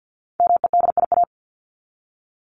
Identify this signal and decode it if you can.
Morse code, heard as a keyed tone, about 50 wpm, 700 Hz. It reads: MEDIA
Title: Top 501-750 Words 50wpm